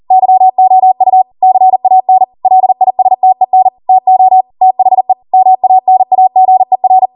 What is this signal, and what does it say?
Morse code, heard as a keyed tone, about 40 wpm, 750 Hz. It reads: YOU CAN LISTEN TO THE MANAGER
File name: Morse / 40WPM